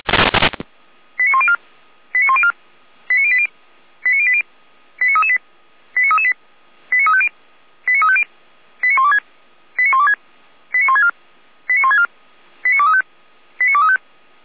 Five-tone selective call (ZVEI), heard as quick runs of five tones, tones of about 70 ms. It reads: 89104 89104 89080 89080 89338 89338 89350 89350 89126 89126 89164 89164 89235 89235